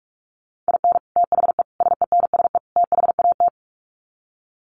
Morse code, heard as a keyed tone, about 44 wpm, 700 Hz. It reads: IN THE SENSE THAT
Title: Sets of 4 Words 45wpm